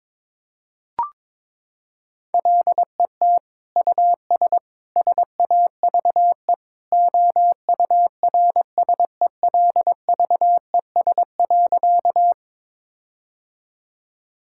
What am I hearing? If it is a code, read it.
Morse code, heard as a keyed tone, about 21 wpm, 700 Hz. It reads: LET US SAVE OURSELVES.
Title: Sentences from Top 1600 Words 22wpm